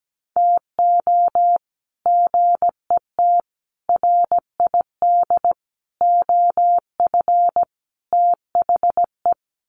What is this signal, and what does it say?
Morse code, heard as a keyed tone, about 17 wpm, 700 Hz. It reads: TO GET RID OF THE
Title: Sets of 5 Words 17wpm